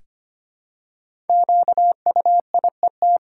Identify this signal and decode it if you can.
Morse code, heard as a keyed tone, about 25 wpm, 700 Hz. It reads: QUIET
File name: 5 Letter Words - First 3 Letters Given - 25wpm